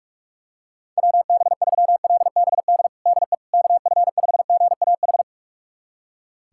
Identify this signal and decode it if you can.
Morse code, heard as a keyed tone, about 41 wpm, 700 Hz. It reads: WB3LBD DE KF5GAH
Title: USA Callsign DE Callsign 45wpm